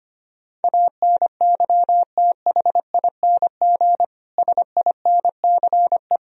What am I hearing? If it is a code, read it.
Morse code, heard as a keyed tone, about 25 wpm, 700 Hz. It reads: ANYTHING SINCE
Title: Sentences from Top 1000 Words - Review Part 1 - 25wpm